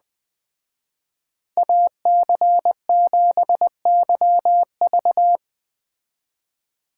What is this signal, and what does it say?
Morse code, heard as a keyed tone, about 20 wpm, 700 Hz. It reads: AC7YV
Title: Speed Racing - USA 2x2 Call Signs 17wpm